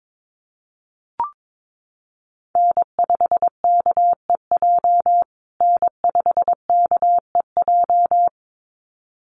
Morse code, heard as a keyed tone, about 22 wpm, 700 Hz. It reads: N5KEJ N5KEJ
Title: USA Call Signs Repeated 2x 22wpm